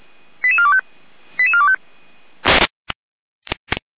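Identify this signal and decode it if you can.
Five-tone selective call (ZVEI), heard as quick runs of five tones, tones of about 70 ms. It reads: 88425 88425
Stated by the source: ZVEI five-tone selective call received on SDR.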